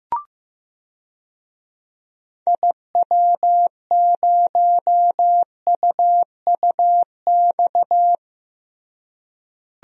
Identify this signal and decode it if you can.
Morse code, heard as a keyed tone, about 15 wpm, 700 Hz. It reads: IW0UUX
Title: DXCC-100 Call Signs - 6 Characters - Repeated 2x - 15wpm